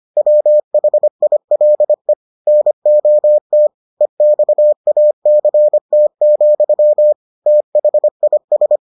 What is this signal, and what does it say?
Morse code, heard as a keyed tone, about 25 wpm, 600 Hz. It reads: WHILE NOT EXACT, THIS